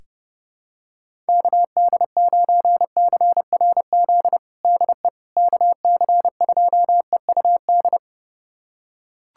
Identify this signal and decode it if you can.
Morse code, heard as a keyed tone, about 30 wpm, 700 Hz. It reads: KD9CRZ DE KC2EUD
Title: USA Callsign DE Callsign 30wpm